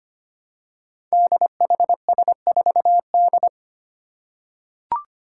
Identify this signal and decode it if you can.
Morse code, heard as a keyed tone, about 25 wpm, 700 Hz. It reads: DHS4D
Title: Five Challenging Letter-Number Combinations 25wpm